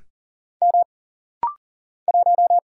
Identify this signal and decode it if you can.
Morse code, heard as a keyed tone, about 40 wpm, 700 Hz. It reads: M 1
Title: Single Letter-Number - Rapid-Fire 40wpm